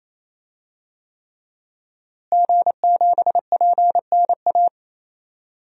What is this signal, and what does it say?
Morse code, heard as a keyed tone, about 28 wpm, 700 Hz. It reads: G7PNA
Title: DXCC-100 Call Signs - 5 Characters - Repeated 3x - 28wpm